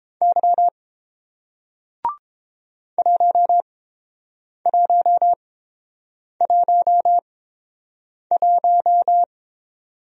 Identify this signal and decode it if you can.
Morse code, heard as a keyed tone, about 30 wpm, spaced as for 7 wpm, 700 Hz. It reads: Y 1111